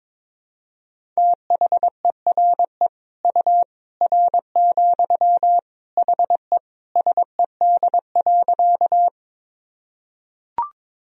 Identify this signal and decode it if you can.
Morse code, heard as a keyed tone, about 22 wpm, 700 Hz. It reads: THERE U R, HE SED.
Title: Sentences from Top 100 Words and Abbreviations 22wpm